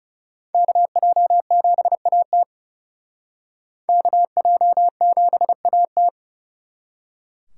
Morse code, heard as a keyed tone, about 36 wpm, 700 Hz. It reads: KJ7AT KJ7AT
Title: Speed Racing - USA 2x2 Call Signs 30wpm